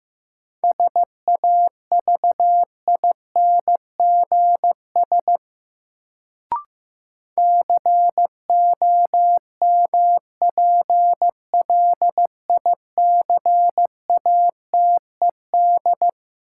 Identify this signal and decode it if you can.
Morse code, heard as a keyed tone, about 15 wpm, 700 Hz. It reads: SAVINGS COMPLICATED
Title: Sentences from Top 1200 Words - New Word Drill Alt - 15wpm